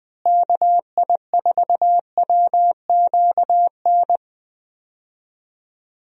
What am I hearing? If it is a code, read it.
Morse code, heard as a keyed tone, about 20 wpm, 700 Hz. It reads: KI4WQN
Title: USA Call Signs - 6 Characters - Repeated 2x - 20wpm